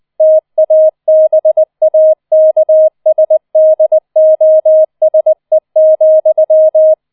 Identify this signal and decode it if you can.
Morse code, heard as a keyed tone, about 20 wpm, 630 Hz. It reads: TABAKSDOSE,